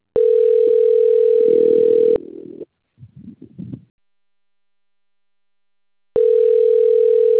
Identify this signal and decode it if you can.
Ringback tone (North America), 440 + 480 Hz.